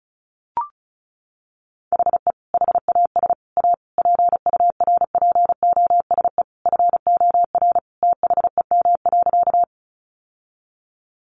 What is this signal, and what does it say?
Morse code, heard as a keyed tone, about 34 wpm, 700 Hz. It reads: HE HAS A PURPOSE FOR THEM.
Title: Sentences from Top 800 Words 35wpm